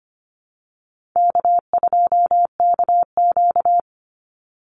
Morse code, heard as a keyed tone, about 25 wpm, 700 Hz. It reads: K2KQ
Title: Call Signs - 25wpm